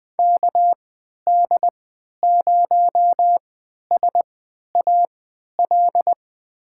Morse code, heard as a keyed tone, about 20 wpm, spaced as for 11 wpm, 700 Hz. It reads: KD0SAL
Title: USA Call Signs - 6 Characters - 3x Character Spacing - 20wpm